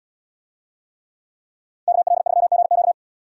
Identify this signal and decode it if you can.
Morse code, heard as a keyed tone, about 77 wpm, 700 Hz. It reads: KL3GQ